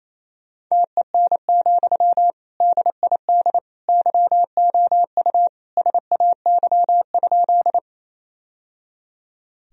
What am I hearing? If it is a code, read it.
Morse code, heard as a keyed tone, about 28 wpm, 700 Hz. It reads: TEN, DID YOU SAY?
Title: Sentences from Top 1000 Words - Review Part 2 - 28wpm